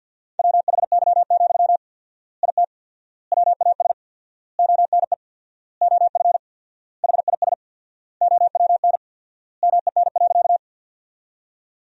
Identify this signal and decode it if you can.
Morse code, heard as a keyed tone, about 53 wpm, 700 Hz. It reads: WHY, IT WAS ONE OF HIS OWN MEN.
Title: 4x Word-Spacing - Sentences from Top 300 Words - 50wpm